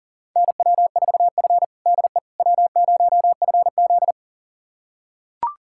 Morse code, heard as a keyed tone, about 40 wpm, 700 Hz. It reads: NW4F DE W0FZ
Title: USA Callsign DE Callsign - 4 Characters - 40wpm